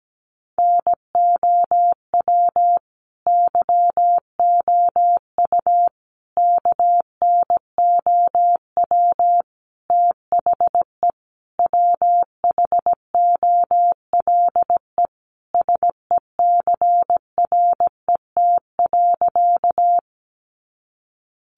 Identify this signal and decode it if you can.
Morse code, heard as a keyed tone, about 17 wpm, 700 Hz. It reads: NOW YOU KNOW THE WHOLE SECRET.
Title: Sentences from Top 1200 Words 17wpm